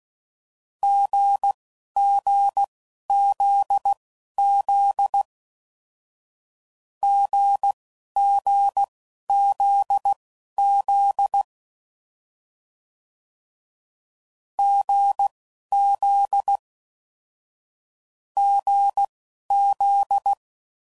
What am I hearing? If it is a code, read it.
Morse code, heard as a keyed tone, about 16 wpm, spaced as for 11 wpm, 800 Hz. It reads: GGZZ GGZZ GZ GZ
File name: Combinations / Difficult Combos / GZ GZ7